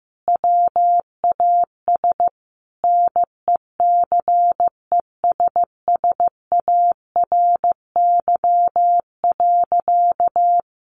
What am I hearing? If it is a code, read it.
Morse code, heard as a keyed tone, about 15 wpm, 700 Hz. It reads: WAS NECESSARY.
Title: Sentences from Top 1500 Words 15wpm